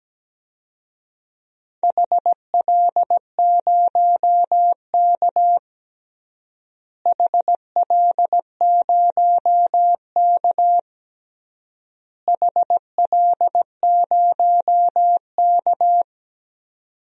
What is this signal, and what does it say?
Morse code, heard as a keyed tone, about 17 wpm, 700 Hz. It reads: HL0K HL0K HL0K
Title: DXCC-100 Call Signs - 4 Characters - Repeated 3x - 17wpm